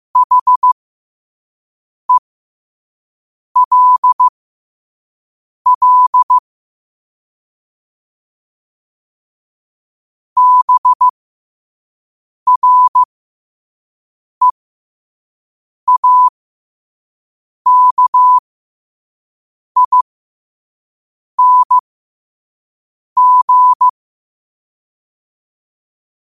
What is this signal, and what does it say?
Morse code, heard as a keyed tone, about 15 wpm, spaced as for 5 wpm, 1000 Hz. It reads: HELL BREAKING
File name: qotdfiles / 5wpm